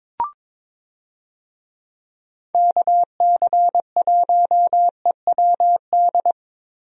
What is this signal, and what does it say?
Morse code, heard as a keyed tone, about 22 wpm, 700 Hz. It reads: KC1EWD
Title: USA Call Signs - 6 Characters - Repeated 2x - 22wpm